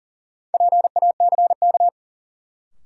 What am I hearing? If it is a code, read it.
Morse code, heard as a keyed tone, about 40 wpm, 700 Hz. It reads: PACK
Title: Sentences from Top 1300 Words - New Word Drill Alt - 40wpm